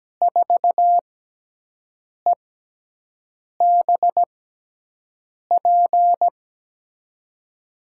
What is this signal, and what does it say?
Morse code, heard as a keyed tone, about 17 wpm, 700 Hz. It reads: 4 E B P
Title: USA Call Signs - 6 Characters - 6x Character Spacing - 17wpm